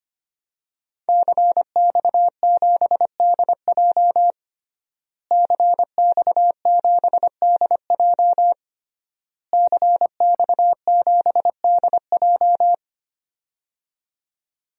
Morse code, heard as a keyed tone, about 25 wpm, 700 Hz. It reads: CX7DJ CX7DJ CX7DJ
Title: DXCC-100 Call Signs - 5 Characters - Repeated 3x - 25wpm